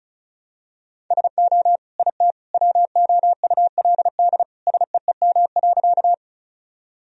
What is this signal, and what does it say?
Morse code, heard as a keyed tone, about 36 wpm, 700 Hz. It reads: SO IT WOULD SEEM.
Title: Sentences from Top 1000 Words - Review Part 1 - 35wpm